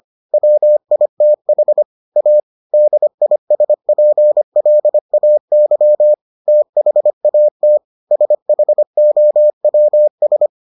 Morse code, heard as a keyed tone, about 25 wpm, 600 Hz. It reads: WITH A DISPLAY THAT SHOWS